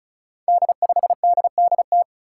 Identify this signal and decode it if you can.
Morse code, heard as a keyed tone, about 36 wpm, 700 Hz. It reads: D5DDT
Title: Five Challenging Letter-Number Combinations 35wpm